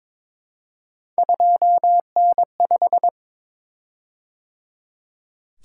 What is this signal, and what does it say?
Morse code, heard as a keyed tone, about 23 wpm, 700 Hz. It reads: 2N5
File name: practice / 22wpm / Three Letter-Number Combinations - 22wpm